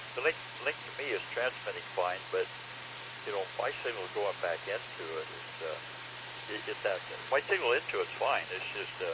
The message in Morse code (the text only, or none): none